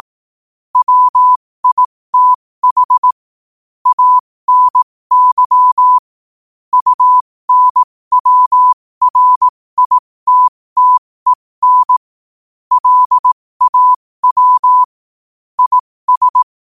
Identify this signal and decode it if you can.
Morse code, heard as a keyed tone, about 18 wpm, spaced as for 15 wpm, 1000 Hz. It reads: WITH ANY UNWRITTEN LAW IS
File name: qotdfiles / 15wpm